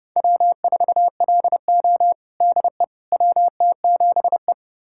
Morse code, heard as a keyed tone, about 30 wpm, 700 Hz. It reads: W4LO DE WT7E
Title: USA Callsign DE Callsign - 4 Characters - 30wpm